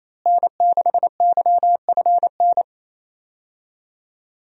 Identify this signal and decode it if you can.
Morse code, heard as a keyed tone, about 28 wpm, 700 Hz. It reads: N6YFN
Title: USA Call Signs - 5 Characters - Repeated 3x - 28wpm